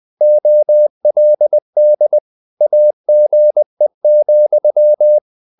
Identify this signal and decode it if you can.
Morse code, heard as a keyed tone, about 20 wpm, 600 Hz. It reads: OLD AGE,
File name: the old man and the sea / the old man and the sea part 1 20wpm